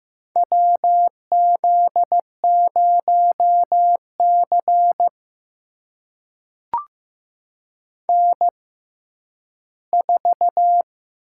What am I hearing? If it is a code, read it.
Morse code, heard as a keyed tone, about 15 wpm, spaced as for 10 wpm, 700 Hz. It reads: WZ0C N4
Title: USA Call Signs - 4 Characters - 6x Character Spacing - 15wpm